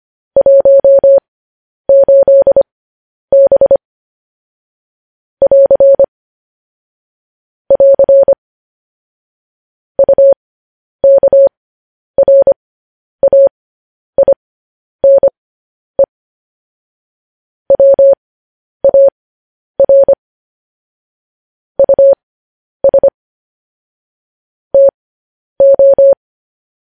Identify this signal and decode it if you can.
Morse code, heard as a keyed tone, about 25 wpm, spaced as for 10 wpm, 550 Hz. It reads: 18B <AR> <AR> UKRAINE WAR US TO
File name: podcast / News headlines-1x-25x10